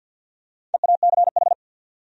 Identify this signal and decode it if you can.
Morse code, heard as a keyed tone, about 53 wpm, 700 Hz. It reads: EACH
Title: Top 100 Words 50wpm